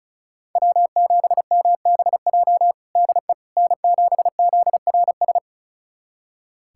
Morse code, heard as a keyed tone, about 36 wpm, 700 Hz. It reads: W7MBJ DE N7ZRS
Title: USA Callsign DE Callsign - 5 Characters - 35wpm